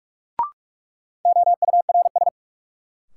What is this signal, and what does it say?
Morse code, heard as a keyed tone, about 46 wpm, 700 Hz. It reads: OURS
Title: Sentences from Top 1000 Words - Review Part 5 - New Word Drill - 45wpm